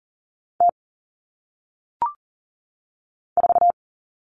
Morse code, heard as a keyed tone, about 40 wpm, 700 Hz. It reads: T 4
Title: Common Cut Numbers - 1-Digit 40wpm